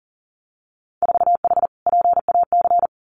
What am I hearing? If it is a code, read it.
Morse code, heard as a keyed tone, about 42 wpm, 700 Hz. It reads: 4H PAC
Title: Winter Field Day Exchanges Encore 40wpm